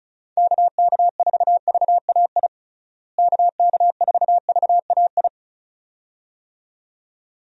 Morse code, heard as a keyed tone, about 35 wpm, 700 Hz. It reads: KK4VAI KK4VAI
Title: USA Call Signs - 6 Characters - Repeated 2x - 35wpm